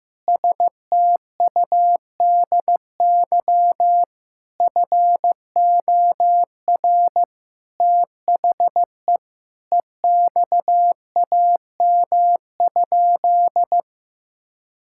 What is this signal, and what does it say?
Morse code, heard as a keyed tone, about 15 wpm, 700 Hz. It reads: STUDY FOR THE EXAM?
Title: Sentences from Top 1000 Words - Review Part 5 - 15wpm